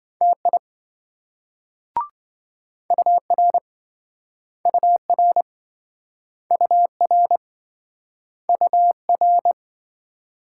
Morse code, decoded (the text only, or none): TI UR UR UR UR